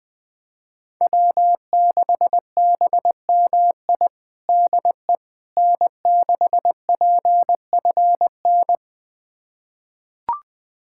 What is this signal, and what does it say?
Morse code, heard as a keyed tone, about 21 wpm, 700 Hz. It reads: W6BMI DE N6PFN
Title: USA Callsign DE Callsign - 5 Characters - 20wpm